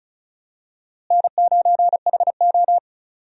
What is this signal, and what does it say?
Morse code, heard as a keyed tone, about 36 wpm, 700 Hz. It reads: N9HO
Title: USA Call Signs - 4 Characters - Repeated 2x - 35wpm